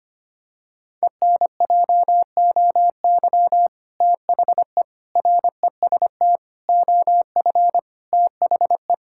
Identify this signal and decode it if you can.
Morse code, heard as a keyed tone, about 25 wpm, 700 Hz. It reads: ENJOY THE REST OF THE
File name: practice / 25wpm / Sentences with 2-Word Phrases - 25wpm